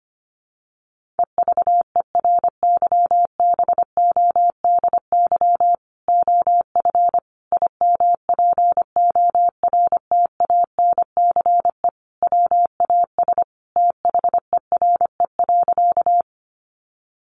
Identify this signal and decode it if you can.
Morse code, heard as a keyed tone, about 25 wpm, 700 Hz. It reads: EVERYBODY OF IMPORTANCE WAS THERE.
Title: Sentences from Top 1400 Words 25wpm